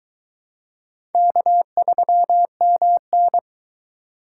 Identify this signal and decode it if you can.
Morse code, heard as a keyed tone, about 21 wpm, 700 Hz. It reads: K3MN